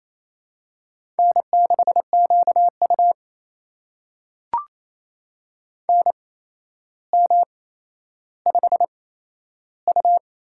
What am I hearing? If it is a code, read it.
Morse code, heard as a keyed tone, about 27 wpm, spaced as for 12 wpm, 700 Hz. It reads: N6QU NM5U
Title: USA Call Signs - 4 Characters - 8x Character Spacing - 28wpm